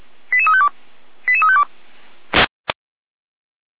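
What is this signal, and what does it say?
Five-tone selective call (ZVEI), heard as quick runs of five tones, tones of about 70 ms. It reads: 88352 88352
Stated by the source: ZVEI five-tone selective call received by SDR.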